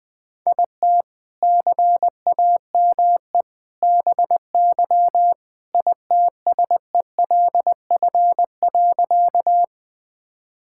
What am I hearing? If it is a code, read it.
Morse code, heard as a keyed tone, about 20 wpm, 700 Hz. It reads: IT CAME BY ITSELF.
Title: Sentences from Top 1000 Words - Review Part 5 - 20wpm